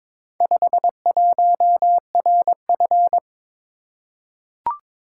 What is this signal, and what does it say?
Morse code, heard as a keyed tone, about 21 wpm, 700 Hz. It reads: 51RF
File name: practice / 22wpm / Four Letter-Number Combinations - 22wpm